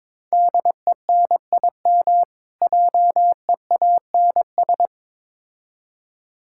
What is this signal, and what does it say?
Morse code, heard as a keyed tone, about 22 wpm, 700 Hz. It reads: DENIM JEANS
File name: Difficult Words with Hints - 22wpm